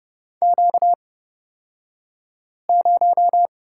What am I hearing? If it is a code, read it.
Morse code, heard as a keyed tone, about 30 wpm, spaced as for 5 wpm, 700 Hz. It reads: Q0
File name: Lesson 38d - Letter q - Two Characters 30wpm